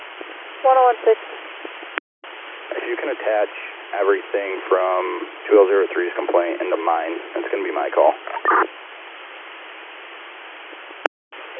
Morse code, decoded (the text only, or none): none